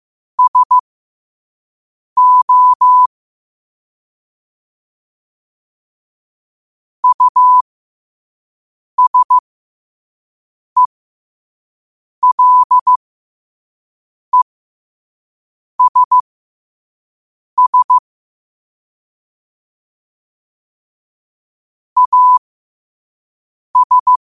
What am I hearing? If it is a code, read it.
Morse code, heard as a keyed tone, about 15 wpm, spaced as for 5 wpm, 1000 Hz. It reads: SO USELESS AS
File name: qotdfiles / 5wpm